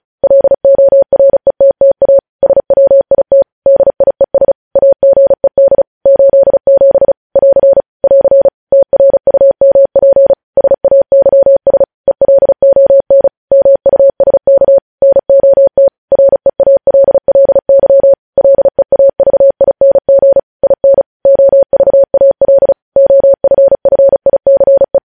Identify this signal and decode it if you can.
Morse code, heard as a keyed tone, about 36 wpm, 550 Hz. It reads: LORETTA SWIT DIES AGED 87 <AR> <AR> TRUMP SAYS ELON MUSK NOT REALLY LEAVING IN OVAL OFFICE
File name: podcast / News headlines-1x-35x35